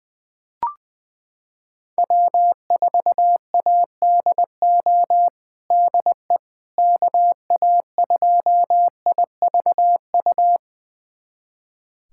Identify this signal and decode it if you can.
Morse code, heard as a keyed tone, about 20 wpm, 700 Hz. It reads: W4ADO DE KA2IVU
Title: USA Callsign DE Callsign 20wpm